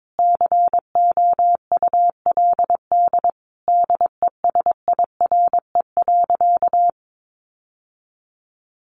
Morse code, heard as a keyed tone, about 22 wpm, 700 Hz. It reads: COULD DESIRE.
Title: Sentences from Top 1100 Words 22wpm